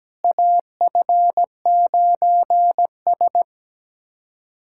Morse code, heard as a keyed tone, about 17 wpm, 700 Hz. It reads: AF9S
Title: USA Callsign DE Callsign - 4 Characters - 17wpm